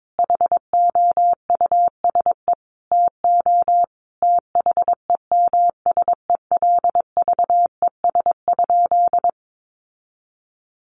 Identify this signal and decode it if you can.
Morse code, heard as a keyed tone, about 22 wpm, 700 Hz. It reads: HOUSE TO THEMSELVES?
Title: Sentences from Top 800 Words 22wpm